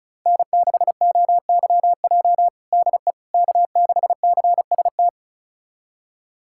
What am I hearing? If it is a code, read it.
Morse code, heard as a keyed tone, about 36 wpm, 700 Hz. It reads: N6OYJ DE K6CST
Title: USA Callsign DE Callsign - 5 Characters - 35wpm